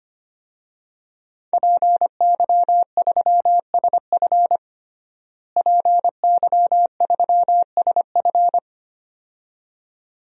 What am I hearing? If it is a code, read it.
Morse code, heard as a keyed tone, about 25 wpm, 700 Hz. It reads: PY3SF PY3SF
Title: DXCC-100 Call Signs - 5 Characters - Repeated 2x - 25wpm